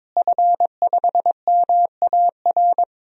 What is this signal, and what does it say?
Morse code, heard as a keyed tone, about 22 wpm, 700 Hz. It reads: F5MAR
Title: DXCC-100 Call Signs - 5 Characters - Repeated 2x - 22wpm